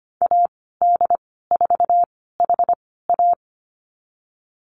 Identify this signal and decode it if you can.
Morse code, heard as a keyed tone, about 25 wpm, spaced as for 16 wpm, 700 Hz. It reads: AD4HA
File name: Lesson 15g - Number 5 - Call Signs 25wpm